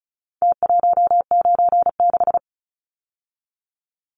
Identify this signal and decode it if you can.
Morse code, heard as a keyed tone, about 34 wpm, 700 Hz. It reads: T196
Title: Numbers - Contest Style - 4-Digit 35wpm